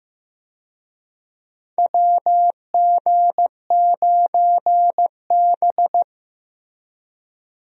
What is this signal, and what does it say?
Morse code, heard as a keyed tone, about 15 wpm, 700 Hz. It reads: WG9B
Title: USA Call Signs - 4 Characters - Repeated 2x - 15wpm